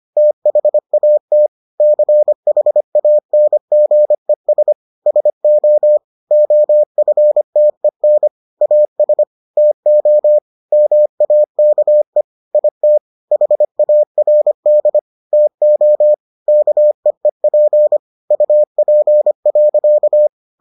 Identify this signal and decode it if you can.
Morse code, heard as a keyed tone, about 25 wpm, 600 Hz. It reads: THAT CHANGES SO OFTEN AS TO MAKE IT HARD TO KEEP UP.